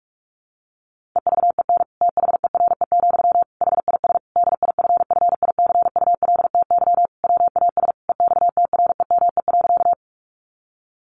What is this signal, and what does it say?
Morse code, heard as a keyed tone, about 44 wpm, 700 Hz. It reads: EVEN THERE, HIS DIFFICULTY WAS EXTREME.
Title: Sentences from Top 1500 Words 45wpm